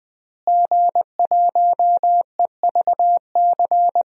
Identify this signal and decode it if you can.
Morse code, heard as a keyed tone, about 21 wpm, 700 Hz. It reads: G1EVC
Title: DXCC-100 Call Signs - 5 Characters - Repeated 3x - 20wpm